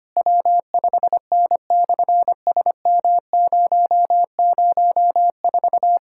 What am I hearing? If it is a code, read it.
Morse code, heard as a keyed tone, about 24 wpm, 700 Hz. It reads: W5N/SM004
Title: SOTA QSO Practice 25wpm